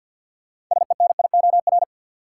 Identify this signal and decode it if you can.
Morse code, heard as a keyed tone, about 52 wpm, 700 Hz. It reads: SENIOR
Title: Sets of 5 Words Encore - New Word Drill - 50wpm